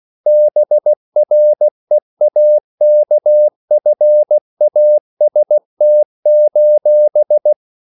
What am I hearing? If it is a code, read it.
Morse code, heard as a keyed tone, about 16 wpm, 600 Hz. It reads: BREAKFAST:
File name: CW / Practice / EnglishQuotations0116